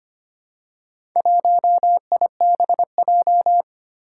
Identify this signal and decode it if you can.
Morse code, heard as a keyed tone, about 25 wpm, 700 Hz. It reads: 1IBJ
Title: Four Challenging Letter-Number Combinations 25wpm